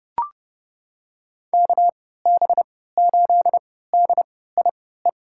Morse code, heard as a keyed tone, about 30 wpm, spaced as for 17 wpm, 700 Hz. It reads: KB8DIE
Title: USA Call Signs - 6 Characters - 3x Character Spacing - 30wpm